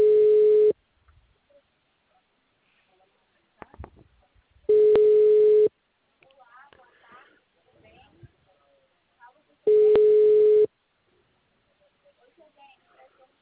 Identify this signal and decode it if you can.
Ringback tone (Europe, 425 Hz), with speech.